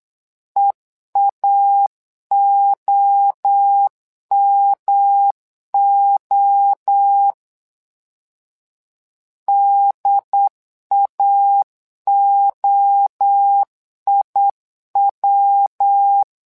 Morse code, heard as a keyed tone, about 8 wpm, 800 Hz. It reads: EAOMO DAOIW
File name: LEZIONI / 3 MWRO / 3 MWRO 8wpm 2606